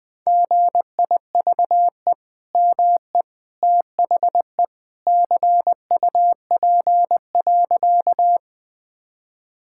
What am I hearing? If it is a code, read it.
Morse code, heard as a keyed tone, about 20 wpm, 700 Hz. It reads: GIVE ME THE CUP.
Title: Sentences from Top 1600 Words 20wpm